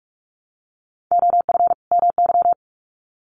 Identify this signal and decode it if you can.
Morse code, heard as a keyed tone, about 44 wpm, 700 Hz. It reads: OF MY
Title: Sets of 2 Words 45wpm